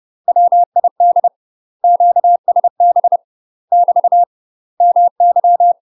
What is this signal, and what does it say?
Morse code, heard as a keyed tone, about 30 wpm, 700 Hz. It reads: WID QSB <BT> MY
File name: UK MASTER QSOs TRAINING / 30WPM / UK MASTER QSOs 0024